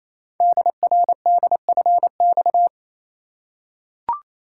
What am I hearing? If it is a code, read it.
Morse code, heard as a keyed tone, about 28 wpm, 700 Hz. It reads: DRDFX
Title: Code Groups - 5 Letters - 28wpm